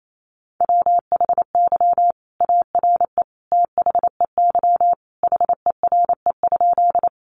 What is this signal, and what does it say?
Morse code, heard as a keyed tone, about 28 wpm, 700 Hz. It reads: WHY ARE THEY HERE?
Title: Sentences from Top 200 Words 28wpm